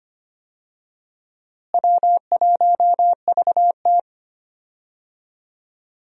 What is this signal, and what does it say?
Morse code, heard as a keyed tone, about 25 wpm, 700 Hz. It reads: W1VT
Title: USA Call Signs - 4 Characters - Repeated 3x - 25wpm